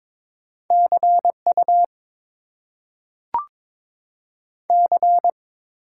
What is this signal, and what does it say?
Morse code, heard as a keyed tone, about 22 wpm, 700 Hz. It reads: CU C
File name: Top 90 QSO Elements - 8x Character Spacing - 22wpm